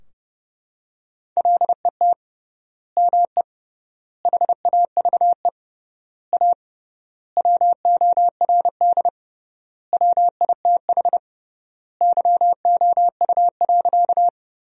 Morse code, heard as a keyed tone, about 30 wpm, 700 Hz. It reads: LET ME HAVE A WORD WITH YOU.
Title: 3x Word-Spacing - Sentences from Top 300 Words - 30wpm